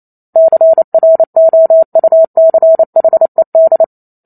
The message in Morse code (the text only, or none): CROUCHED